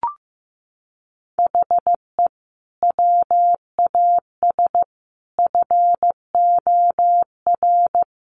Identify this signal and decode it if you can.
Morse code, heard as a keyed tone, about 15 wpm, 700 Hz. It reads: HE WAS FOR
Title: Sentences from Top 1400 Words 15wpm